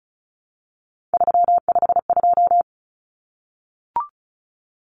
Morse code, heard as a keyed tone, about 34 wpm, 700 Hz. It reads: 352
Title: Numbers - 3-Digit 35wpm